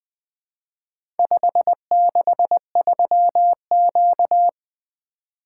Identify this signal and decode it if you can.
Morse code, heard as a keyed tone, about 20 wpm, 700 Hz. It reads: 563Q
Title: Four Letter-Number Combinations - 20wpm